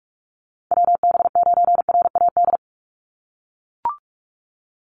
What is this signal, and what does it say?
Morse code, heard as a keyed tone, about 44 wpm, 700 Hz. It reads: WB9RAD
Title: USA Call Signs 45wpm